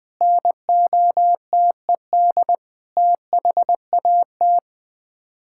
Morse code, heard as a keyed tone, about 21 wpm, 700 Hz. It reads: NOTED THAT
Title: Sets of 6 Words - 20wpm